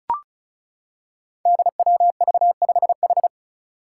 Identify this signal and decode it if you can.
Morse code, heard as a keyed tone, about 34 wpm, 700 Hz. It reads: DWV5H
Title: Five Challenging Letter-Number Combinations 35wpm